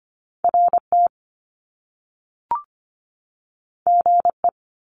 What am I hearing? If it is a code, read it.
Morse code, heard as a keyed tone, about 25 wpm, 700 Hz. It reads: RT GE
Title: Top Two Letter Combinations in Words 25wpm